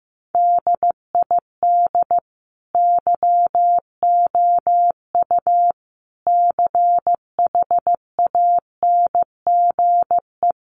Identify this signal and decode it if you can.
Morse code, heard as a keyed tone, about 15 wpm, 700 Hz. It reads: DID YOU CHANGE
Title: Sentences from Top 1400 Words 15wpm